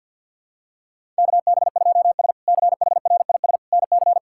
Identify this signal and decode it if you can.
Morse code, heard as a keyed tone, about 48 wpm, 700 Hz. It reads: KB2S CHRIS NC
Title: K1USN SST QSO Practice - 50wpm